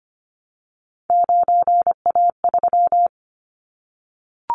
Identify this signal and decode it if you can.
Morse code, heard as a keyed tone, about 25 wpm, 700 Hz. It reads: 9A3
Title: Callsign Prefixes for Top 25 DXCC - Repeated 3x - 25wpm